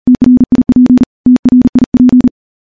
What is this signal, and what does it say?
Morse code, heard as a keyed tone, about 40 wpm, 250 Hz. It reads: TRIP TRIP
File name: morse-training-40WPM